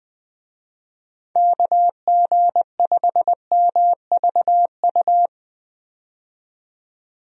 Morse code, heard as a keyed tone, about 21 wpm, 700 Hz. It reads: KG5MVU
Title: USA Call Signs - 6 Characters - 20wpm